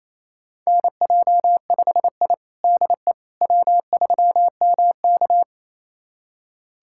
Morse code, heard as a keyed tone, about 27 wpm, 700 Hz. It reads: NJ5I DE W3MK
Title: USA Callsign DE Callsign - 4 Characters - 28wpm